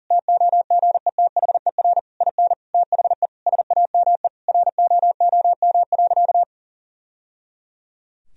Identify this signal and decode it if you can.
Morse code, heard as a keyed tone, about 40 wpm, 700 Hz. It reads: TOGETHER IN THE SAME ROOM.
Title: Sentences from Top 1000 Words - Review Part 3 - 40wpm